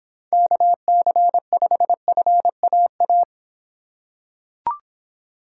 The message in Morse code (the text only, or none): KC5FAA